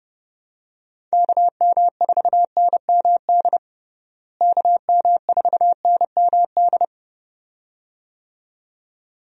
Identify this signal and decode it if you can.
Morse code, heard as a keyed tone, about 30 wpm, 700 Hz. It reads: KM4NMD KM4NMD
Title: USA Call Signs - 6 Characters - Repeated 2x - 30wpm